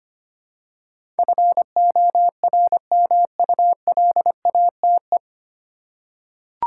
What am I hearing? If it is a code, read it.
Morse code, heard as a keyed tone, about 25 wpm, 700 Hz. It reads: FORMULATE
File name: Nine Letter Words - 25wpm